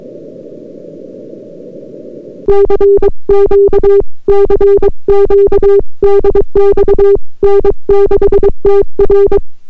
Morse code, heard as a keyed tone, about 22 wpm, 400 Hz. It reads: CQ CQ DX N6TR